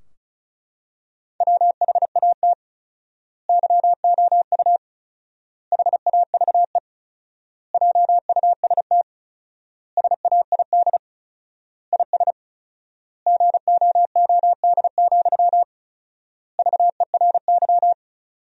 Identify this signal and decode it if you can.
Morse code, heard as a keyed tone, about 36 wpm, 700 Hz. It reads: WHAT YOU HAVE JUST SAID IS GOOD, VERY
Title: 4x Word-Spacing - Sentences from Top 200 Words - 35wpm